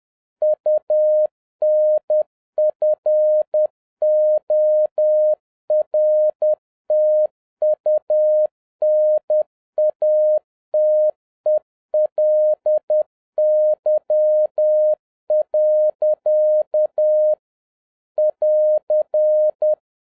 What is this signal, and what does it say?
Morse code, heard as a keyed tone, about 10 wpm, 610 Hz. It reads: UNFORTUNATELY. <AR>